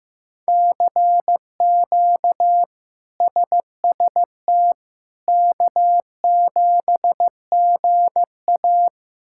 Morse code, heard as a keyed tone, about 15 wpm, 700 Hz. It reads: CQ SST K7GA
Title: K1USN SST QSO Practice - 15wpm